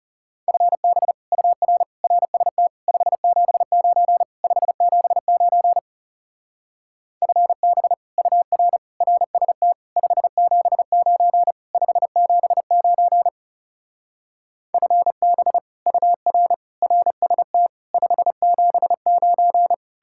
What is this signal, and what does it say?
Morse code, heard as a keyed tone, about 36 wpm, 700 Hz. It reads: FB UR RST 579 579 FB UR RST 579 579 FB UR RST 579
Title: Speed-Racing Common QSO Phrases - 30wpm